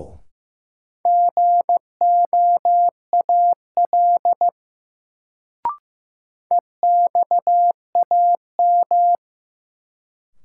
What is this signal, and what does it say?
Morse code, heard as a keyed tone, about 15 wpm, 700 Hz. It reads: GOAL EXAM
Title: Sentences from Top 1000 Words - Review Part 5 - New Word Drill - 15wpm